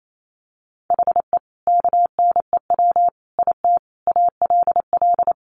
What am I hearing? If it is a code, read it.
Morse code, heard as a keyed tone, about 28 wpm, 700 Hz. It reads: HE KNEW IT ALL
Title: Sentences from Top 1200 Words 28wpm